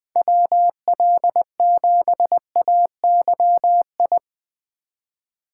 Morse code, heard as a keyed tone, about 19 wpm, 700 Hz. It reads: WL7AYI
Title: Speed Racing - USA Call Signs 17wpm